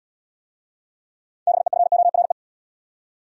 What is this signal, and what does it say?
Morse code, heard as a keyed tone, about 69 wpm, 700 Hz. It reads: BEFORE